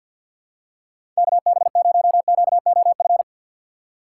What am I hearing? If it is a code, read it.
Morse code, heard as a keyed tone, about 51 wpm, 700 Hz. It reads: KB0QOF
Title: USA Call Signs - 6 Characters - 6x Character Spacing - 50wpm